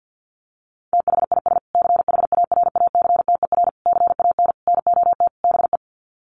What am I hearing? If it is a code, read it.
Morse code, heard as a keyed tone, about 50 wpm, 700 Hz. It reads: THIS CHARACTER CAN NOT BE
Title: Sets of 5 Words Encore2 50wpm